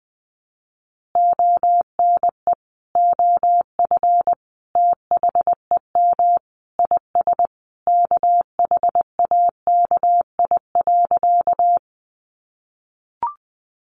Morse code, heard as a keyed tone, about 20 wpm, 700 Hz. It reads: ONE OF THEM IS KHAKI.
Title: Additional Easy Phrases 20wpm